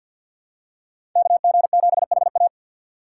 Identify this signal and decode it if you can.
Morse code, heard as a keyed tone, about 52 wpm, 700 Hz. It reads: KG7HA
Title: USA Call Signs - 5 Characters - 3x Character Spacing - 50wpm